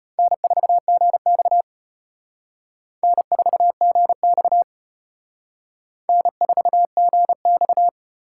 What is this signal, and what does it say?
Morse code, heard as a keyed tone, about 31 wpm, 700 Hz. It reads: N4GX N4GX N4GX
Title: Speed Racing - USA 1x2 Call Signs 25wpm